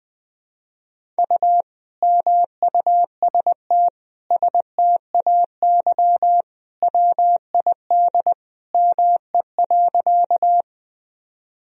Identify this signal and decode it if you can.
Morse code, heard as a keyed tone, about 20 wpm, 700 Hz. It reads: U MUST STAY WID ME.
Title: Sentences from Top 500 Words and Abbreviations 20wpm